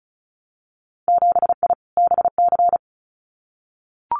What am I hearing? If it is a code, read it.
Morse code, heard as a keyed tone, about 34 wpm, 700 Hz. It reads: 7I BC
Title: Winter Field Day Exchanges Encore 35wpm